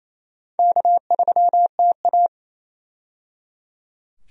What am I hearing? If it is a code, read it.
Morse code, heard as a keyed tone, about 28 wpm, 700 Hz. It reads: K3TA
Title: USA Call Signs - 4 Characters - 28wpm